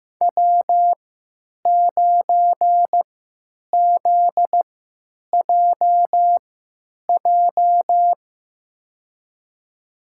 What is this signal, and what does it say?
Morse code, heard as a keyed tone, about 15 wpm, spaced as for 9 wpm, 700 Hz. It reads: W9ZJJ
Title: USA Call Signs - 5 Characters - 3x Character Spacing - 15wpm